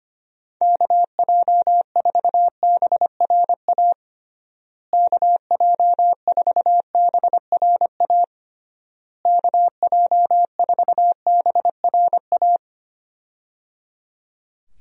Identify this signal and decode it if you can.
Morse code, heard as a keyed tone, about 25 wpm, 700 Hz. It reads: KJ4BRA KJ4BRA KJ4BRA
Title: USA Call Signs - 6 Characters - Repeated 3x - 25wpm